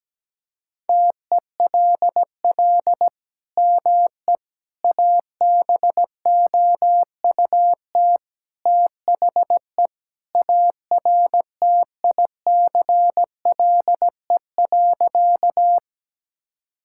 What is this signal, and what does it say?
Morse code, heard as a keyed tone, about 17 wpm, 700 Hz. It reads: TELL ME ABOUT THE ARTICLE.
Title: Sentences from Top 1000 Words - Review Part 5 - 17wpm